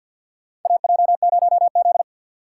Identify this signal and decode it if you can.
Morse code, heard as a keyed tone, about 48 wpm, 700 Hz. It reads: AJ0Z
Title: USA Call Signs - 4 Characters - 6x Character Spacing - 50wpm